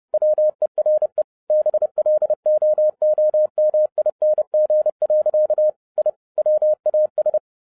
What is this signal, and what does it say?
Morse code, heard as a keyed tone, about 30 wpm, 610 Hz. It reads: WERE BLOOMING. I WAS
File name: 2011-07-mp3 / n55 BACK FROM A BIG ACCIDENT 30wpm